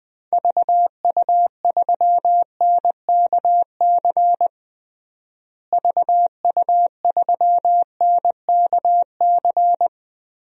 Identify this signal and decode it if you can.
Morse code, heard as a keyed tone, about 20 wpm, 700 Hz. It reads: VU3NKC VU3NKC
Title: DXCC-100 Call Signs - 6 Characters - Repeated 3x - 20wpm